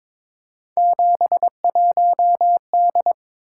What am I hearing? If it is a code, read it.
Morse code, heard as a keyed tone, about 22 wpm, 700 Hz. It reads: 71D
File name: practice / 22wpm / Three Letter-Number Combinations - 22wpm